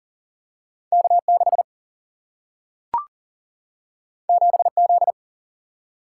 Morse code, heard as a keyed tone, about 40 wpm, 700 Hz. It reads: K 6 7 Z
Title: Single-Letter Single-Number 40wpm